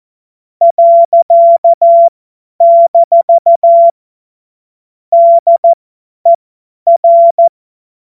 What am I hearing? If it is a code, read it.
Morse code, heard as a keyed tone, about 14 wpm, spaced as for 10 wpm, 680 Hz. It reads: . - DER
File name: Morsekurs / Witze / Witz 20. Juli 2021, 50 F70